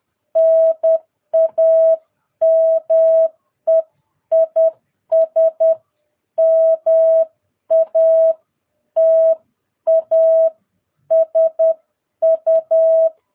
Morse code, heard as a keyed tone, about 10 wpm, spaced as for 9 wpm, 650 Hz. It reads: NAME IS MATASU
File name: jh8cbh / 2022cw